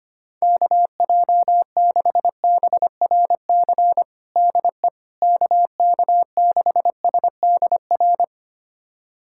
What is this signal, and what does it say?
Morse code, heard as a keyed tone, about 25 wpm, 700 Hz. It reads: KJ6BRC DE KK6SDR
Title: USA Callsign DE Callsign - 6 Characters - 25wpm